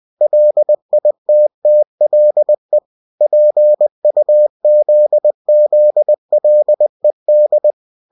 Morse code, heard as a keyed tone, about 20 wpm, 600 Hz. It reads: LITTLE PUZZLED
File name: cw-books / winnie-the-pooh / winnie-20wpm-page-0047